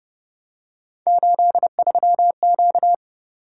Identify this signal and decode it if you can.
Morse code, heard as a keyed tone, about 29 wpm, 700 Hz. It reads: 83Q